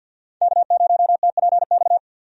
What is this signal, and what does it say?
Morse code, heard as a keyed tone, about 47 wpm, 700 Hz. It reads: K0TPX
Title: USA Call Signs - 5 Characters - Repeated 2x - 50wpm